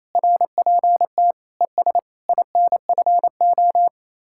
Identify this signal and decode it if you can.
Morse code, heard as a keyed tone, about 35 wpm, 700 Hz. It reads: RPT ES INFO
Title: Speed-Racing Common QSO Phrases Encore - 28wpm